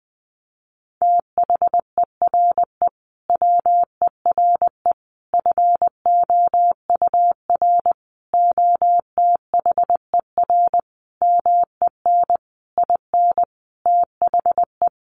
Morse code, heard as a keyed tone, about 20 wpm, 700 Hz. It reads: THERE WERE FOUR OTHER MEN IN THE
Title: A Sound Of Thunder 20wpm